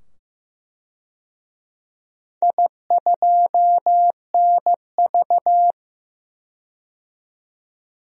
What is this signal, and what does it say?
Morse code, heard as a keyed tone, about 15 wpm, 700 Hz. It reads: I2NV
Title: DXCC-100 Call Signs - 4 Characters - 15wpm